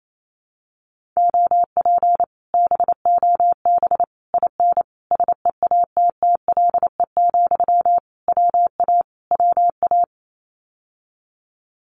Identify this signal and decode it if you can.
Morse code, heard as a keyed tone, about 28 wpm, 700 Hz. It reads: OP BOB IN SEATTLE, WA WA
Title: Common QSO Phrases Encore 28wpm